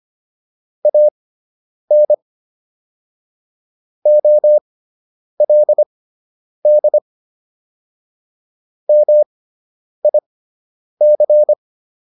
Morse code, heard as a keyed tone, about 25 wpm, spaced as for 9 wpm, 600 Hz. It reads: AN OLD MIC